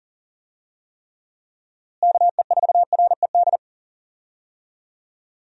Morse code, heard as a keyed tone, about 40 wpm, 700 Hz. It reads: KE4RED
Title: USA Call Signs - 6 Characters - Repeated 3x - 40wpm